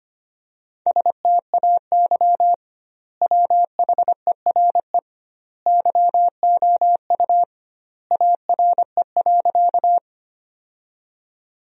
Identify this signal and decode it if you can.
Morse code, heard as a keyed tone, about 25 wpm, 700 Hz. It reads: STAY WHERE YOU ARE.
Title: 2x Word-Spacing - Sentences from Top 500 Words - 25wpm